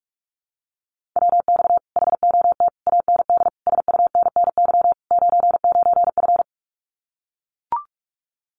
Morse code, heard as a keyed tone, about 44 wpm, 700 Hz. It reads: WX HOT AND SUNNY 99F
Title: Common QSO Phrases Encore 45wpm